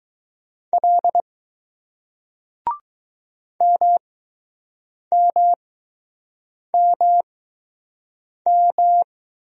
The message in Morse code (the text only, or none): L MMMM